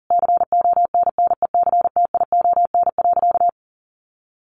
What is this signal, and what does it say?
Morse code, heard as a keyed tone, about 40 wpm, 700 Hz. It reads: CONNECTION.
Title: Sentences from Top 1300 Words 40wpm